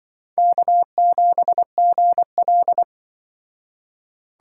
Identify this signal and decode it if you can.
Morse code, heard as a keyed tone, about 25 wpm, 700 Hz. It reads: K7GL
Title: Speed Racing - USA 1x2 Call Signs 20wpm